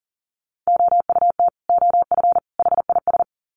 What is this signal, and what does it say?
Morse code, heard as a keyed tone, about 40 wpm, 700 Hz. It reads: OUT OF HIS
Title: Sets of 3 Words 40wpm